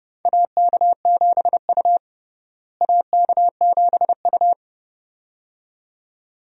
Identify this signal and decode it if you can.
Morse code, heard as a keyed tone, about 30 wpm, 700 Hz. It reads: AK7U AK7U
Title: USA Call Signs - 4 Characters - Repeated 2x - 30wpm